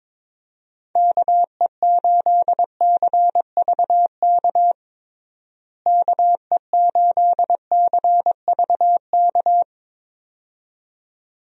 Morse code, heard as a keyed tone, about 21 wpm, 700 Hz. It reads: KE8CVK KE8CVK
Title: USA Call Signs - 6 Characters - Repeated 2x - 22wpm